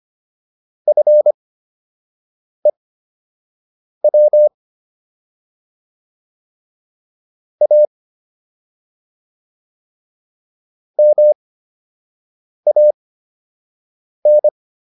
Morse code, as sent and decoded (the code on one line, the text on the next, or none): ..-. .-- / .- / -- .- -.
FEW A MAN